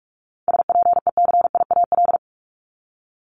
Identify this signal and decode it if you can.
Morse code, heard as a keyed tone, about 46 wpm, 700 Hz. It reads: SPECIAL
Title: Multi-letter Phonemes 45wpm